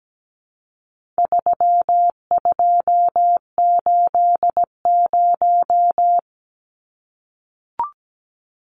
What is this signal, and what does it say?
Morse code, heard as a keyed tone, about 17 wpm, 700 Hz. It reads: 3280
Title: Numbers - 4-Digit 17wpm